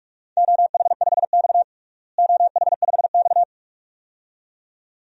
Morse code, heard as a keyed tone, about 46 wpm, 700 Hz. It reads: OH5X OH5X
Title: DXCC-100 Call Signs - 4 Characters - Repeated 2x - 45wpm